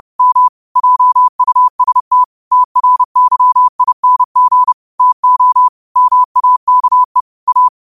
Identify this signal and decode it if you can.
Morse code, heard as a keyed tone, about 30 wpm, 1000 Hz. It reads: M JUST TRYING TO MAKE A
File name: qotdfiles / 30wpm